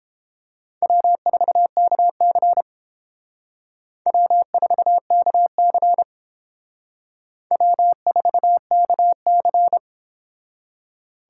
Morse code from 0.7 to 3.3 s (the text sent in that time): W4KC